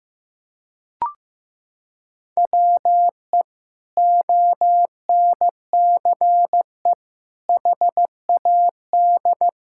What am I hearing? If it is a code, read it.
Morse code, heard as a keyed tone, about 15 wpm, 700 Hz. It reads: WE ONCE HAD
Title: Sentences from Top 1000 Words - Review Part 6 - 15wpm